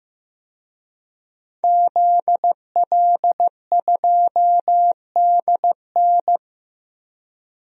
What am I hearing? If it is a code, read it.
Morse code, heard as a keyed tone, about 15 wpm, 700 Hz. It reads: ZL2DN
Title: DXCC-100 Call Signs - 5 Characters - Repeated 2x - 15wpm